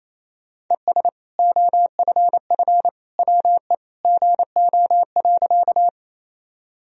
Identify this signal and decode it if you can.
Morse code, heard as a keyed tone, about 27 wpm, 700 Hz. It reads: ES OFF WE GO.
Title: Sentences from Top 200 Words and Abbreviations 28wpm